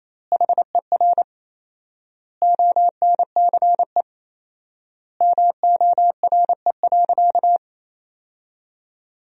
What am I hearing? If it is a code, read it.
Morse code, heard as a keyed tone, about 28 wpm, 700 Hz. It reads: HER ONCE MORE.
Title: 4x Word-Spacing - Sentences from Top 500 Words - 28wpm